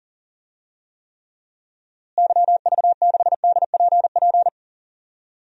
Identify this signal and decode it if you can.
Morse code, heard as a keyed tone, about 40 wpm, 700 Hz. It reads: YV6DPP
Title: DXCC-100 Call Signs - 6 Characters - Repeated 2x - 40wpm